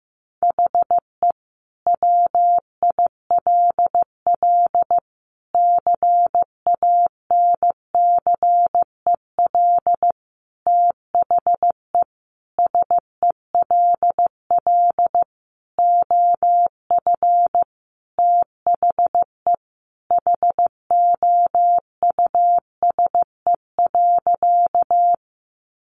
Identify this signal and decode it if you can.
Morse code, heard as a keyed tone, about 15 wpm, 700 Hz. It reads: HE WILL CANCEL THE SELL OF THE HOUSE.
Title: Sentences from Top 1200 Words 15wpm